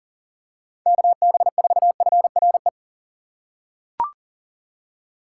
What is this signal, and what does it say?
Morse code, heard as a keyed tone, about 40 wpm, 700 Hz. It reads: KB4FRE
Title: USA Call Signs - 6 Characters - 2x Character Spacing - 40wpm